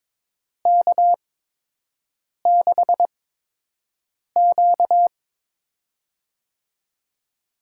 Morse code, heard as a keyed tone, about 22 wpm, spaced as for 6 wpm, 700 Hz. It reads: K6Q
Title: USA Call Signs - 4 Characters - 8x Character Spacing - 22wpm